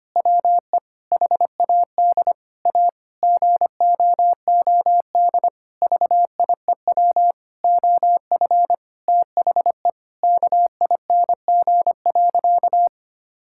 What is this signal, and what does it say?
Morse code, heard as a keyed tone, about 25 wpm, 700 Hz. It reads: WE HAD A GOOD VIEW OF THE KING.
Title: Sentences from Top 1000 Words - Review Part 6 - 25wpm